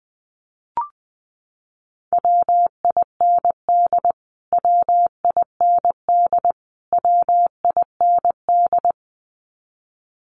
Morse code, heard as a keyed tone, about 20 wpm, 700 Hz. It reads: WIND WIND WIND
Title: Top 500 Words Repeated 20wpm